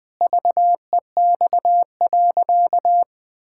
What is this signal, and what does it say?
Morse code, heard as a keyed tone, about 21 wpm, 700 Hz. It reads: VEX.
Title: Pangram Challenge v2 - 20wpm